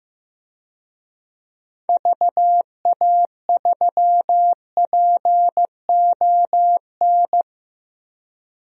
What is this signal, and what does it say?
Morse code, heard as a keyed tone, about 15 wpm, 700 Hz. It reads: VA3PON
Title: DXCC-100 Call Signs - 6 Characters - Repeated 3x - 15wpm